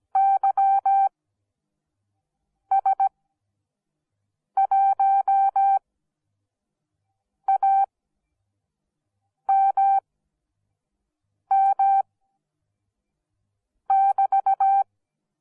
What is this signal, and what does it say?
Morse code, heard as a keyed tone, about 17 wpm, spaced as for 5 wpm, 780 Hz. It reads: YS1AMM<BT>